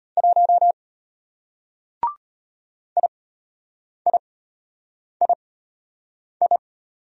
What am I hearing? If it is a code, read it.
Morse code, heard as a keyed tone, about 33 wpm, spaced as for 8 wpm, 700 Hz. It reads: 1 IIII